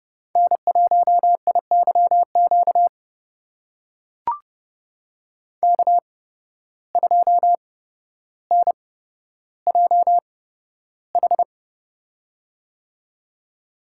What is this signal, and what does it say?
Morse code, heard as a keyed tone, about 31 wpm, spaced as for 13 wpm, 700 Hz. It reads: N1IYQ K2NJH
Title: USA Call Signs - 5 Characters - 8x Character Spacing - 30wpm